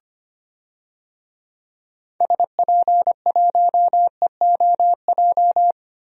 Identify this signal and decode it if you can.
Morse code, heard as a keyed tone, about 25 wpm, 700 Hz. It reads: SP1EOJ
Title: DXCC-100 Call Signs - 6 Characters - Repeated 3x - 25wpm